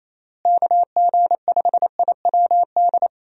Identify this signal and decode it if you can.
Morse code, heard as a keyed tone, about 27 wpm, 700 Hz. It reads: KG5IWD
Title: USA Call Signs - 6 Characters - Repeated 2x - 28wpm